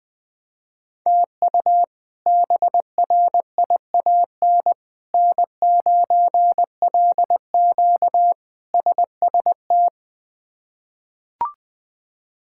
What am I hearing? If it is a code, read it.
Morse code, heard as a keyed tone, about 20 wpm, 700 Hz. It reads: TU BRIAN N9LQ SST
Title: K1USN SST QSO Practice - 20wpm